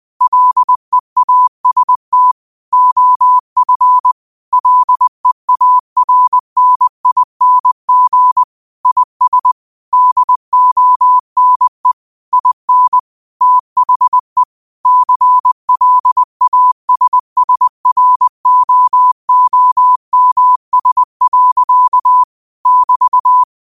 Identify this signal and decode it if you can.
Morse code, heard as a keyed tone, about 20 wpm, 1000 Hz. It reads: LEAST OF LEARNING IS DONE IN THE CLASSROOMS. <BT>